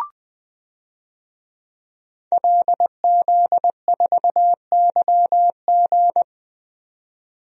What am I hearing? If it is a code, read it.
Morse code, heard as a keyed tone, about 20 wpm, 700 Hz. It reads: LZ4YG
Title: DXCC-100 Call Signs - 5 Characters - 20wpm